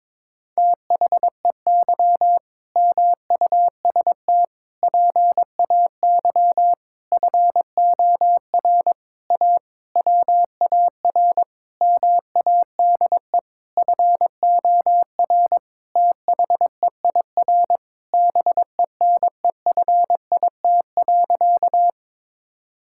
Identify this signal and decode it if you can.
Morse code, heard as a keyed tone, about 22 wpm, 700 Hz. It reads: THEY MUST PAY FOR A WAR MADE FOR THEIR BENEFIT.
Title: Sentences from Top 1000 Words - Review Part 6 - 22wpm